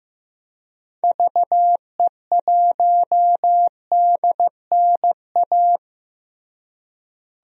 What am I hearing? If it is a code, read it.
Morse code, heard as a keyed tone, about 15 wpm, 700 Hz. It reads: VE1DNA
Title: DXCC-100 Call Signs - 6 Characters - Repeated 3x - 15wpm